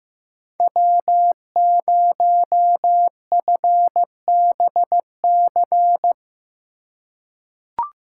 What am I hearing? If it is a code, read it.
Morse code, heard as a keyed tone, about 15 wpm, 700 Hz. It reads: W0FBC
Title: USA Call Signs - 5 Characters - 4x Character Spacing - 15wpm